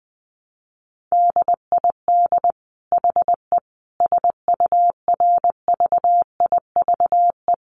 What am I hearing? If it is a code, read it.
Morse code, heard as a keyed tone, about 20 wpm, 700 Hz. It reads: DID HE SURVIVE
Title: Sentences from Top 1300 Words 20wpm